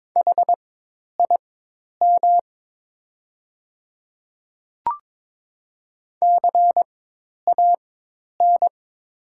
Morse code, heard as a keyed tone, about 22 wpm, spaced as for 10 wpm, 700 Hz. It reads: HIM CAN
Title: 4x Character Spacing - Top 100 Words - 22wpm